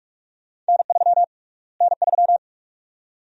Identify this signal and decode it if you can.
Morse code, heard as a keyed tone, about 43 wpm, 700 Hz. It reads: N3 N3
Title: USA Call Sign Prefixes - Repeated 2x - 45wpm